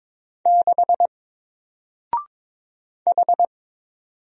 Morse code, heard as a keyed tone, about 22 wpm, 700 Hz. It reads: 6 H
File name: Single-Letter-Number-22wpm